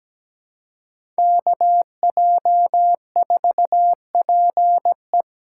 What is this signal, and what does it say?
Morse code, heard as a keyed tone, about 17 wpm, 700 Hz. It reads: KJ4PE
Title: USA Call Signs - 5 Characters - 4x Character Spacing - 17wpm